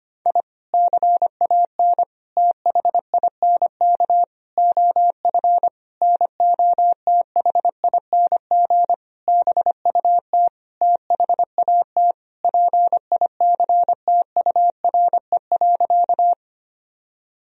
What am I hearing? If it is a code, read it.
Morse code, heard as a keyed tone, about 25 wpm, 700 Hz. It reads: I CAN THINK OF NOTHING BUT THAT PICTURE.
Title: Sentences from Top 1000 Words - Review Part 1 - 25wpm